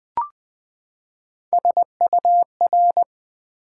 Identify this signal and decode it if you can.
Morse code, heard as a keyed tone, about 20 wpm, 700 Hz. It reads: SUR
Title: Build A Word - 3 Letters At A Time - 8-Letter Words - 20wpm